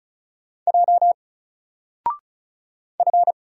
Morse code, heard as a keyed tone, about 36 wpm, 700 Hz. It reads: J F
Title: Single Letter-Number Alt 35wpm